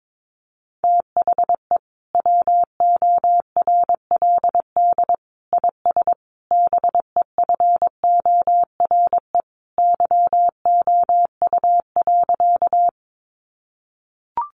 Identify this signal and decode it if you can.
Morse code, heard as a keyed tone, about 22 wpm, 700 Hz. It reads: THE WORLD IS BEFORE YOU.
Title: Sentences from Top 200 Words 22wpm